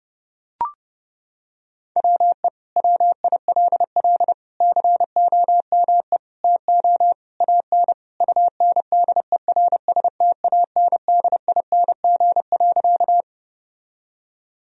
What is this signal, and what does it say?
Morse code, heard as a keyed tone, about 30 wpm, 700 Hz. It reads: WE WILL COME TO AN UNDERSTANDING.
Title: Sentences from Top 1000 Words - Review Part 3 - 30wpm